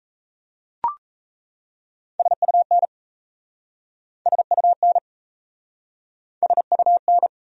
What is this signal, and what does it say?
Morse code, heard as a keyed tone, about 38 wpm, 700 Hz. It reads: SUN SUN SUN